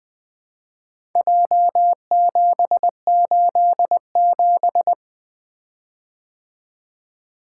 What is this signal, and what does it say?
Morse code, heard as a keyed tone, about 23 wpm, 700 Hz. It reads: J787
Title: Speed-Racing - Four Letter-Number Combinations - 20wpm